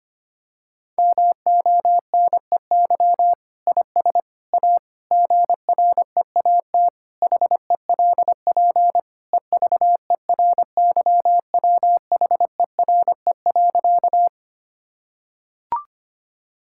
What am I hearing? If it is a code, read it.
Morse code, heard as a keyed tone, about 25 wpm, 700 Hz. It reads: MONEY IS A GREAT HELP EVERYWHERE.
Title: Sentences from Top 1600 Words 25wpm